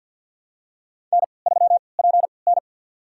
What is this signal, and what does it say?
Morse code, heard as a keyed tone, about 52 wpm, spaced as for 29 wpm, 700 Hz. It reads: N3PN
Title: USA Call Signs - 3x Character Spacing - 50wpm